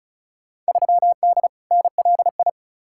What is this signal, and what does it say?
Morse code, heard as a keyed tone, about 36 wpm, 700 Hz. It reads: 3D NLI
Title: Field Day Exchanges Encore - 35wpm